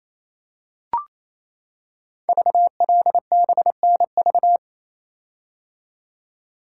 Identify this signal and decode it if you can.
Morse code, heard as a keyed tone, about 27 wpm, 700 Hz. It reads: VLBNV
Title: Code Groups - 5 Letter:Numbers - 28wpm